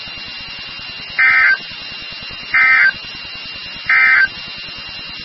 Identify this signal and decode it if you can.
Emergency Alert System (SAME) tones, heard as harsh buzzing bursts. It is heard from 1.1 to 4.4 s.
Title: NOAA Weather Radio WXK36